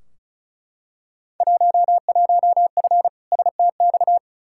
Morse code, heard as a keyed tone, about 36 wpm, 700 Hz. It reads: 11F STX
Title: Field Day Exchanges Encore - 35wpm